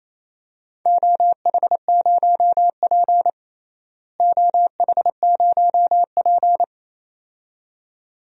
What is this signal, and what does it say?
Morse code, heard as a keyed tone, about 28 wpm, 700 Hz. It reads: OH0P OH0P
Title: DXCC-100 Call Signs - 4 Characters - Repeated 2x - 28wpm